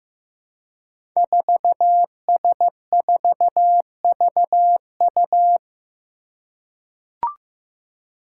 Morse code, heard as a keyed tone, about 15 wpm, 700 Hz. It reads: 4S4VU
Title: Five Challenging Letter-Number Combinations 15wpm